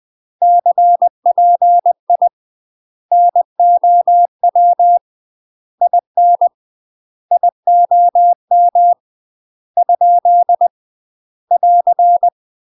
Morse code, heard as a keyed tone, about 20 wpm, 700 Hz. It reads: CPI NOW IN IOM ? <AR>
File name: UK MASTER QSOs TRAINING/20WPM